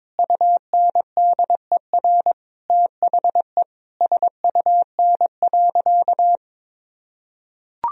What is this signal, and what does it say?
Morse code, heard as a keyed tone, about 22 wpm, 700 Hz. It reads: UNDER THE SUN.
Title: Sentences from Top 1000 Words - Review Part 1 - 22wpm